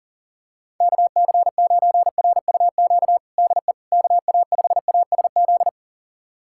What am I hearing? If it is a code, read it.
Morse code, heard as a keyed tone, about 40 wpm, 690 Hz. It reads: KC9RUQ DE KA5ASZ
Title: USA Callsign DE Callsign - 6 Characters - 40wpm